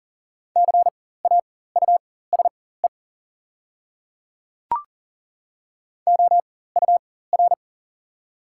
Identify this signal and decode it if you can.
Morse code, heard as a keyed tone, about 40 wpm, spaced as for 19 wpm, 690 Hz. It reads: CAUSE OUR
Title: 4x Character Spacing - Top 300 Words - 40wpm